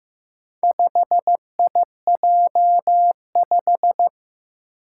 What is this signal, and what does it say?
Morse code, heard as a keyed tone, about 15 wpm, 700 Hz. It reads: 5IJ5
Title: Four Challenging Letter-Number Combinations 15wpm